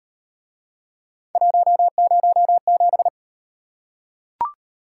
Morse code, heard as a keyed tone, about 38 wpm, 700 Hz. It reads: 107